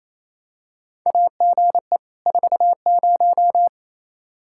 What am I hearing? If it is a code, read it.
Morse code, heard as a keyed tone, about 27 wpm, 700 Hz. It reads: AGE 40
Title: Practice QSO - Exchange 3 - More Info - 28wpm